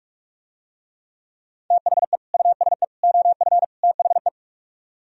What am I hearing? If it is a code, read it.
Morse code, heard as a keyed tone, about 47 wpm, 700 Hz. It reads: THE USE OF THE